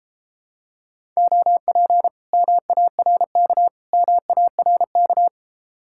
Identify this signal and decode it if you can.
Morse code, heard as a keyed tone, about 29 wpm, 700 Hz. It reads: OP MARK MARK